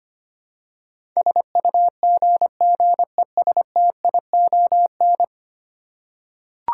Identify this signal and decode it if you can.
Morse code, heard as a keyed tone, about 25 wpm, 700 Hz. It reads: SUGGESTION
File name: Ten Letter Words - 25wpm